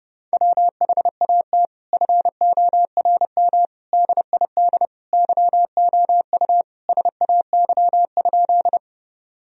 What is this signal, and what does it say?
Morse code, heard as a keyed tone, about 30 wpm, 700 Hz. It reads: WHAT FORM DID YOU SAY?
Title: Sentences from Top 1000 Words - Review Part 1 - 30wpm